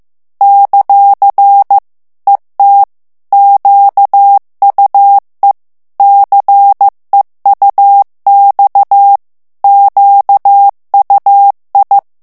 Morse code, heard as a keyed tone, about 15 wpm, 800 Hz. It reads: ; ET QUE CEUX QUI
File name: morse mp3 / FRANCAIS TEXTE CLAIR 2 VITESSE 14 MOTS MINUTES